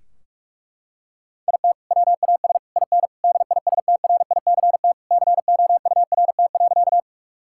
Morse code, heard as a keyed tone, about 46 wpm, 700 Hz. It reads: IT WAS IN DISTRICT COURT.
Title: Sentences from Top 1600 Words 45wpm